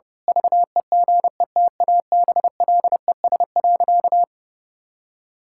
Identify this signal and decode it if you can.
Morse code, heard as a keyed tone, about 32 wpm, 700 Hz. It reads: VEGETABLES.
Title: Pangram Challenge v2 - 30wpm